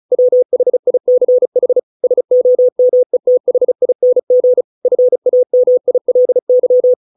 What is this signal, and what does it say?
Morse code, heard as a keyed tone, about 36 wpm, 500 Hz. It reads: WHICH SOMETHING FAMILY